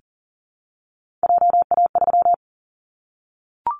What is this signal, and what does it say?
Morse code, heard as a keyed tone, about 38 wpm, 700 Hz. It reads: JA3
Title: Callsign Prefixes for Top 25 DXCC - Repeated 3x - 40wpm